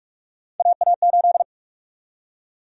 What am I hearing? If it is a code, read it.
Morse code, heard as a keyed tone, about 42 wpm, 700 Hz. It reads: AA8
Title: USA Call Sign Prefixes - 45wpm